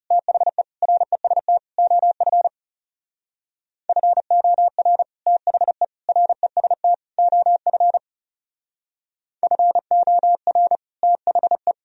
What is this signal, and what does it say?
Morse code, heard as a keyed tone, about 35 wpm, 700 Hz. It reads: THE REST OF FOR THE REST OF FOR THE
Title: Speed-Racing - Sets of 4 Words 30wpm